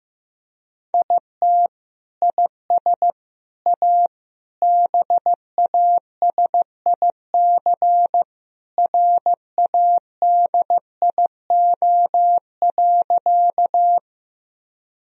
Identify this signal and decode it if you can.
Morse code, heard as a keyed tone, about 15 wpm, 700 Hz. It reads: IT IS A BASIC RADIO.
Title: Sentences from Top 1000 Words - Review Part 3 - 15wpm